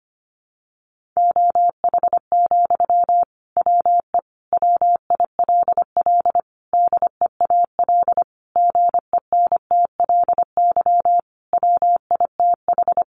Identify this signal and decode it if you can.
Morse code, heard as a keyed tone, about 25 wpm, 700 Hz. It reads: OH, WE WILL DEAL GENTLY WITH
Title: Sentences from Top 1500 Words 25wpm